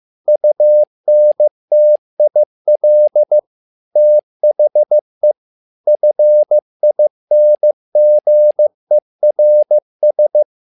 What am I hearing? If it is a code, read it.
Morse code, heard as a keyed tone, about 15 wpm, 600 Hz. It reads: UNTIL THE FINGERS
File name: learning center / tower trouble / tower trouble 15wpm